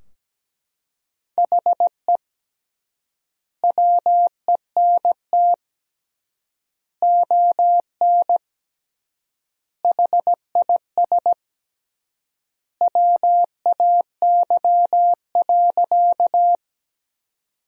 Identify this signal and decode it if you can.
Morse code, heard as a keyed tone, about 17 wpm, 700 Hz. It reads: HE WENT ON HIS WAY.
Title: 3x Word-Spacing - Sentences from Top 200 Words - 17wpm